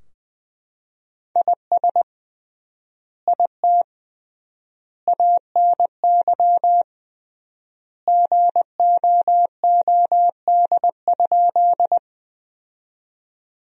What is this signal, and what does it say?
Morse code, heard as a keyed tone, about 20 wpm, 700 Hz. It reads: IS IT ANY GOOD?
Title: 3x Word-Spacing - Sentences from Top 200 Words - 20wpm